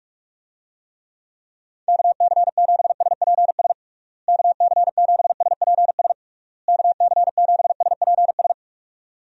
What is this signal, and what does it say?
Morse code, heard as a keyed tone, about 46 wpm, 700 Hz. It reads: KC7SPS KC7SPS KC7SPS
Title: USA Call Signs - 6 Characters - Repeated 3x - 45wpm